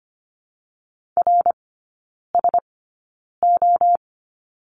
Morse code, heard as a keyed tone, about 25 wpm, spaced as for 9 wpm, 690 Hz. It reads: RSO
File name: Lesson 15e - Number 5 - Three Characters 25wpm